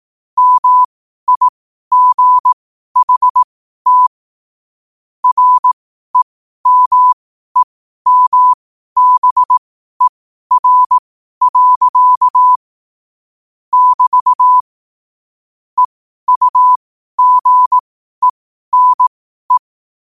Morse code, heard as a keyed tone, about 18 wpm, spaced as for 12 wpm, 1000 Hz. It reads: MIGHT REMEMBER. <BT> EUGENE
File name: qotdfiles / 12wpm